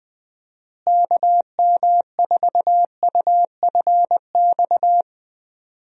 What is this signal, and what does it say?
Morse code, heard as a keyed tone, about 20 wpm, 700 Hz. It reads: KM4UFX
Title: USA Call Signs - 6 Characters - Repeated 2x - 20wpm